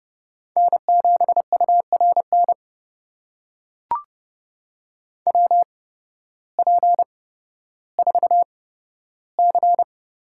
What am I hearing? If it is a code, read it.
Morse code, heard as a keyed tone, about 31 wpm, spaced as for 14 wpm, 700 Hz. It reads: N7URN WP4C
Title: USA Call Signs - 5 Characters - 8x Character Spacing - 30wpm